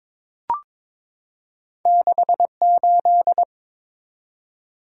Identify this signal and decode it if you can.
Morse code, heard as a keyed tone, about 22 wpm, 700 Hz. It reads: 68
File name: practice / 22wpm / Two Letter-Number Combinations - 22wpm